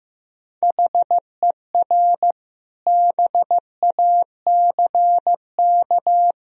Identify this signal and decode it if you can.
Morse code, heard as a keyed tone, about 15 wpm, 700 Hz. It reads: HER BACK
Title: Sentences from Top 1000 Words - Review Part 1 - 15wpm